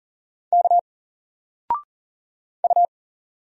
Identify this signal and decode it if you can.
Morse code, heard as a keyed tone, about 40 wpm, 700 Hz. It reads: K U
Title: Single Letter-Number 40wpm